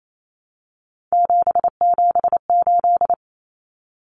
Morse code, heard as a keyed tone, about 28 wpm, 700 Hz. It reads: 778
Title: Common Cut Numbers - 3-Digit 28wpm